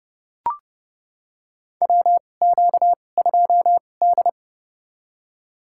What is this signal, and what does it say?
Morse code, heard as a keyed tone, about 29 wpm, spaced as for 21 wpm, 700 Hz. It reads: WQ2D
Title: USA Call Signs - 4 Characters - 2x Character Spacing - 30wpm